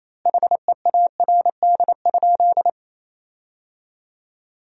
Morse code, heard as a keyed tone, about 28 wpm, 700 Hz. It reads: HEARD?
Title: 4x Word-Spacing - Sentences from Top 400 Words - 28wpm